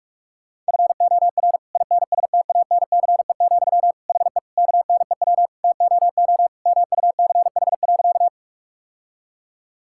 Morse code, heard as a keyed tone, about 46 wpm, 700 Hz. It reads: FOR INSTANCE, HE KNEW TOO MUCH.
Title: Sentences from Top 1000 Words - Review Part 5 - 45wpm